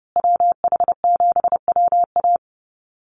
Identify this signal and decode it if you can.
Morse code, heard as a keyed tone, about 30 wpm, 700 Hz. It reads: WH7WA
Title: USA Call Signs Repeated 2x 30wpm